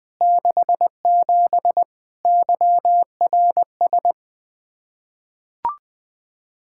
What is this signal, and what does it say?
Morse code, heard as a keyed tone, about 20 wpm, 700 Hz. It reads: 67 YRS
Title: Practice QSO - Exchange 3 - More Info - 20wpm